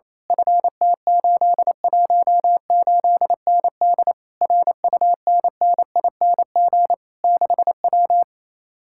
Morse code, heard as a keyed tone, about 27 wpm, 700 Hz. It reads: FT818ND RUNNING 6W
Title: Speed-Racing Common QSO Phrases - 28wpm